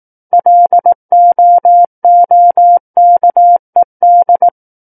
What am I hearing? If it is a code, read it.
Morse code, heard as a keyed tone, about 18 wpm, 700 Hz. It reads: LOOKED
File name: Book One / 09-Chapter 9 - 18 WPM